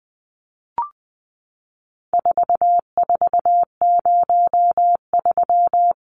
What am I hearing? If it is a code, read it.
Morse code, heard as a keyed tone, about 20 wpm, 700 Hz. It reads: 4403
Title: Numbers - 4-Digit 20wpm